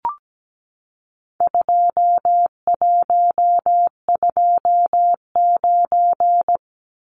Morse code, heard as a keyed tone, about 17 wpm, 700 Hz. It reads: 2129
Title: Numbers - 4-Digit 17wpm